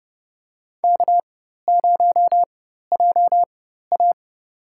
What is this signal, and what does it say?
Morse code, heard as a keyed tone, about 31 wpm, spaced as for 14 wpm, 700 Hz. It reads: K0JA
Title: USA Call Signs - 4 Characters - 4x Character Spacing - 30wpm